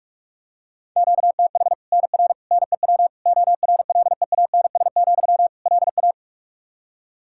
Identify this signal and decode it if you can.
Morse code, heard as a keyed tone, about 41 wpm, 700 Hz. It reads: QTH NR NEW ORLEANS, LA
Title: K1USN SST QSO Practice 45wpm